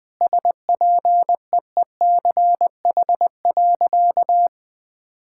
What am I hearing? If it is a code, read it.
Morse code, heard as a keyed tone, about 20 wpm, 700 Hz. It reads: SPEECH.
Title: Sentences with 5-Word Phrases - 20wpm